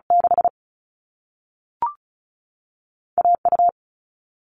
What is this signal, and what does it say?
Morse code, heard as a keyed tone, about 36 wpm, spaced as for 3 wpm, 690 Hz. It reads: 6A U
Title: All Cut Numbers - 2-Digit 35wpm